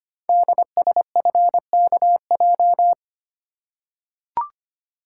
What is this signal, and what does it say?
Morse code, heard as a keyed tone, about 25 wpm, 700 Hz. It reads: DSFKJ
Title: Code Groups - 5 Letters - 25wpm